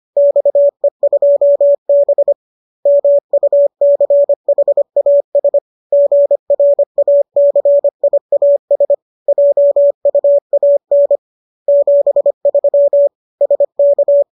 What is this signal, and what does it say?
Morse code, heard as a keyed tone, about 25 wpm, 570 Hz. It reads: XE2B MUCHAS GRACIAS JUAN 73 SK
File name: qso204 25